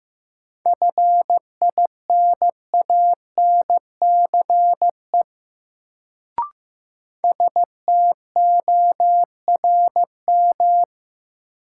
Morse code, heard as a keyed tone, about 15 wpm, 700 Hz. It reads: FINANCE STORM
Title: Sentences from Top 1500 Words - New Word Drill - 15wpm